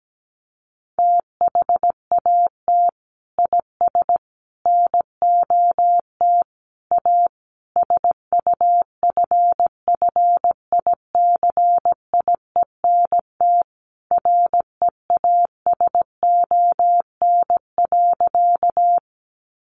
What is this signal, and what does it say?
Morse code, heard as a keyed tone, about 17 wpm, 700 Hz. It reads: THAT IS NOT A SUFFICIENT REASON.
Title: Sentences from Top 1400 Words 17wpm